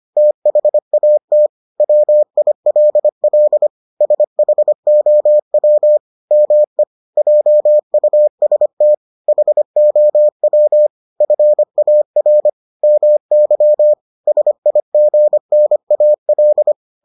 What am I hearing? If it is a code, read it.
Morse code, heard as a keyed tone, about 25 wpm, 600 Hz. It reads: THAT WILL SHOW ME JUST HOW FAR MY SIGNAL